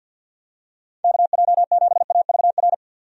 Morse code, heard as a keyed tone, about 45 wpm, 700 Hz. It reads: KJ7AVR
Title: USA Call Signs - 6 Characters - 50wpm